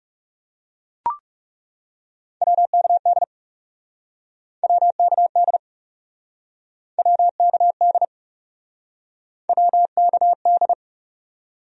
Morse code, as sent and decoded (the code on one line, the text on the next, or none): .-- -.- -.. / .-- -.- -.. / .-- -.- -.. / .-- -.- -..
WKD WKD WKD WKD